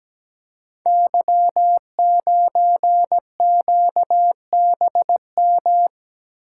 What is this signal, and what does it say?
Morse code, heard as a keyed tone, about 17 wpm, 700 Hz. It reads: Y9QBM
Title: Code Groups - 5 Letter:Numbers - 17wpm